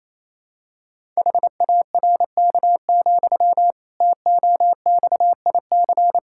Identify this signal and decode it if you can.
Morse code, heard as a keyed tone, about 27 wpm, 700 Hz. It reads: HARK, TOXIC
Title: Pangram Challenge v2 - 28wpm